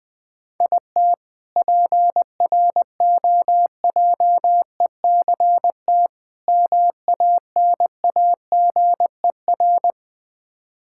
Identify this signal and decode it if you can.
Morse code, heard as a keyed tone, about 20 wpm, 700 Hz. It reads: IT PROJECT MANAGER
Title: K1USN SST QSO Practice 20wpm